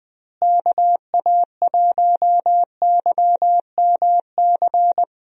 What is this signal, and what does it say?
Morse code, heard as a keyed tone, about 20 wpm, 700 Hz. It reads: KA1YMC
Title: USA Call Signs - 6 Characters - Repeated 3x - 20wpm